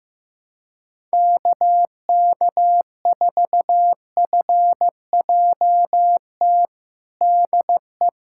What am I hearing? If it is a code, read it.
Morse code, heard as a keyed tone, about 15 wpm, 700 Hz. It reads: KK4FJT DE
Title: USA Callsign DE Callsign - 6 Characters - 15wpm